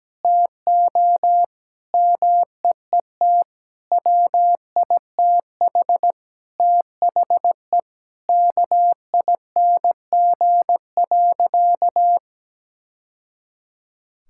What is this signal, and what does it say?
Morse code, heard as a keyed tone, about 17 wpm, 700 Hz. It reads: TO MEET WITH THE KING.
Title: Sentences from Top 1000 Words - Review Part 6 - 17wpm